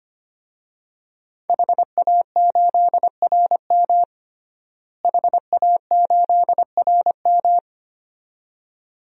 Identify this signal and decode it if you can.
Morse code, heard as a keyed tone, about 25 wpm, 700 Hz. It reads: HA8RM HA8RM
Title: DXCC-100 Call Signs - 5 Characters - Repeated 2x - 25wpm